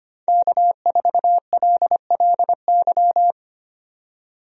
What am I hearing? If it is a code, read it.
Morse code, heard as a keyed tone, about 25 wpm, 700 Hz. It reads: K4LLY
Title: USA Call Signs - 5 Characters - 25wpm